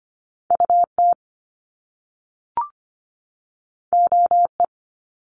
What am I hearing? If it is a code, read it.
Morse code, heard as a keyed tone, about 25 wpm, 700 Hz. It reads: UT OE
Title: Top Two Letter Combinations in Words 25wpm